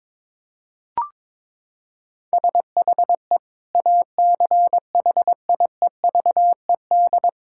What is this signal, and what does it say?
Morse code, heard as a keyed tone, about 22 wpm, 700 Hz. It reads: SHE ACHIEVED
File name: Sentences with 7-Word Phrases - 22wpm